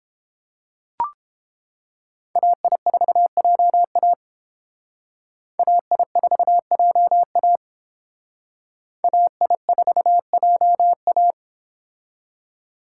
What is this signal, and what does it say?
Morse code, heard as a keyed tone, about 28 wpm, 700 Hz. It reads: AI4JA AI4JA AI4JA